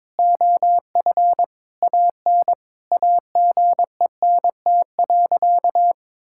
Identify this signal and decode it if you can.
Morse code, heard as a keyed tone, about 22 wpm, 700 Hz. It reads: OF AN AGENT.
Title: Sentences from Top 1600 Words 22wpm